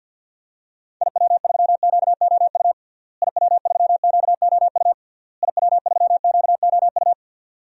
Morse code, heard as a keyed tone, about 50 wpm, 700 Hz. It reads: IW3QOU IW3QOU IW3QOU
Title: DXCC-100 Call Signs - 6 Characters - Repeated 3x - 50wpm